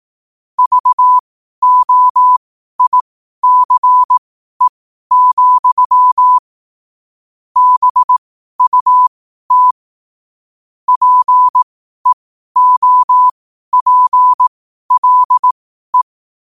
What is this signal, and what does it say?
Morse code, heard as a keyed tone, about 18 wpm, spaced as for 12 wpm, 1000 Hz. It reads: VOICE, BUT PEOPLE
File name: qotdfiles / 12wpm